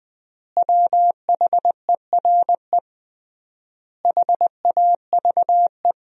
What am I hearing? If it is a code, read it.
Morse code, heard as a keyed tone, about 20 wpm, 700 Hz. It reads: WHERE HAVE
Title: 3x Word-Spacing - Sentences from Top 400 Words - 20wpm